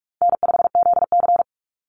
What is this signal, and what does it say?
Morse code, heard as a keyed tone, about 44 wpm, 700 Hz. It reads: N5ZC
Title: NA SOTA Chaser Callsigns 45wpm